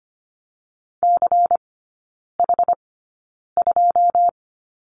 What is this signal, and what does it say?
Morse code, heard as a keyed tone, about 25 wpm, spaced as for 9 wpm, 700 Hz. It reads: CH2
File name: course / 25:10wpm / 093 Lesson 16e - Letter c - Three Characters 25wpm